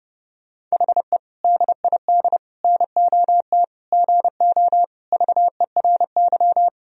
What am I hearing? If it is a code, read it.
Morse code, heard as a keyed tone, about 30 wpm, 700 Hz. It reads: HE DID NOT GO VERY
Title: Sentences from Top 1000 Words - Review Part 2 - 30wpm